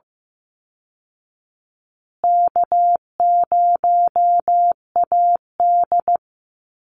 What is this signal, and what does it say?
Morse code, heard as a keyed tone, about 14 wpm, 700 Hz. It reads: K0AD
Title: Call Signs Repeated 3x - 15wpm